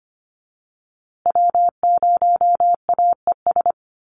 Morse code, heard as a keyed tone, about 25 wpm, 700 Hz. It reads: W0AES
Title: USA Call Signs Repeated 3x 25wpm